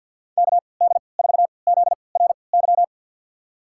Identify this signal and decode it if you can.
Morse code, heard as a keyed tone, about 50 wpm, spaced as for 28 wpm, 700 Hz. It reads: KD4ZRY
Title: USA Call Signs - 6 Characters - 3x Character Spacing - 50wpm